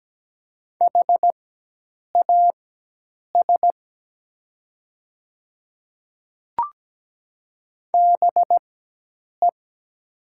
Morse code, heard as a keyed tone, about 17 wpm, spaced as for 8 wpm, 700 Hz. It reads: HAS BE
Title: 4x Character Spacing - Top 100 Words - 17wpm